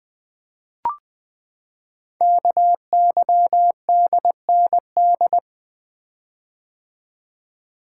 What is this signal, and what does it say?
Morse code, heard as a keyed tone, about 20 wpm, 700 Hz. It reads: KYDND
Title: Code Groups - 5 Letters - 20wpm